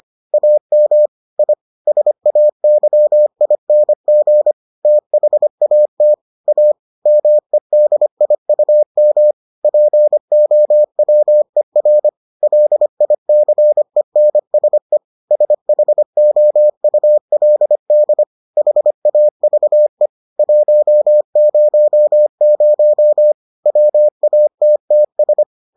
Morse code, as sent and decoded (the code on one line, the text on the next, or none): .- -- / .. / ... .- -.-- .. -. --. / - .... .- - / .- / -- . -.. .. ..- -- / .--. --- .-- . .-. / .-.. .. -.-. -. ... / ... .... --- ..- .-.. -.. / .... .- ...- . / .---- ----- ----- / .-- .- - - ...
AM I SAYING THAT A MEDIUM POWER LICENSE SHOULD HAVE 100 WATTS